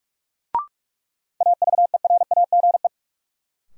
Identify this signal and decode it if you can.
Morse code, heard as a keyed tone, about 46 wpm, 700 Hz. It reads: AVERAGE
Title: Sentences from Top 1400 Words - New Word Drill Alt - 45wpm